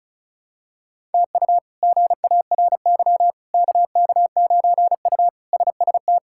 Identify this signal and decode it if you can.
Morse code, heard as a keyed tone, about 36 wpm, 700 Hz. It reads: TU GARY KK9U SST
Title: K1USN SST QSO Practice - 35wpm